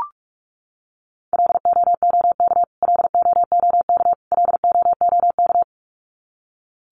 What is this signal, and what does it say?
Morse code, heard as a keyed tone, about 44 wpm, 700 Hz. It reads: LOOK LOOK LOOK
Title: Top 500 Words Repeated 45wpm